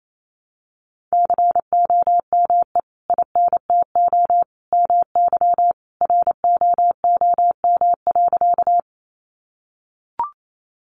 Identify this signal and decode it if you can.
Morse code, heard as a keyed tone, about 28 wpm, 700 Hz. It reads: COME INTO MY ROOM.
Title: Sentences from Top 600 Words 28wpm